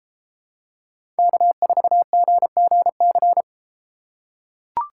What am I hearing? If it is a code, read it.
Morse code, heard as a keyed tone, about 31 wpm, 700 Hz. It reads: K4GGC